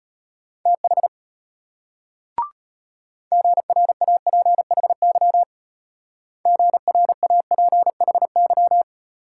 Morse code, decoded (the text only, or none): TH GRAPHY GRAPHY